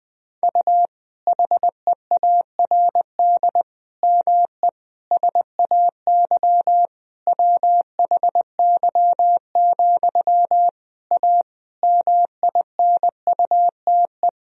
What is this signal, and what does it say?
Morse code, heard as a keyed tone, about 20 wpm, 700 Hz. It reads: U HEARD ME SAY WHY, A MINUTE
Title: Sentences from Top 500 Words and Abbreviations 20wpm